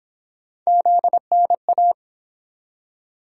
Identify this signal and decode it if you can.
Morse code, heard as a keyed tone, about 29 wpm, 700 Hz. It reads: ZNA